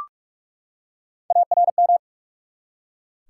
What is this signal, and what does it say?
Morse code, heard as a keyed tone, about 44 wpm, 700 Hz. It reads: ARM
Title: Three Letter Words - 45wpm